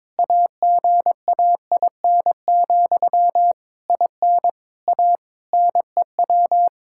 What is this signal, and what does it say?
Morse code, heard as a keyed tone, about 22 wpm, 700 Hz. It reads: AGAIN, IN A NEW
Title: Sentences from Top 1000 Words - Review Part 5 - 22wpm